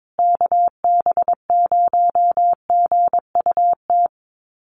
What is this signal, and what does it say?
Morse code, heard as a keyed tone, about 22 wpm, 700 Hz. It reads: KB0GUT
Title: USA Call Signs Repeated 2x 22wpm